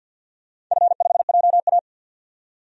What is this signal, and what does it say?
Morse code, heard as a keyed tone, about 52 wpm, 700 Hz. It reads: F5JA
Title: DXCC-100 Call Signs - 4 Characters - 50wpm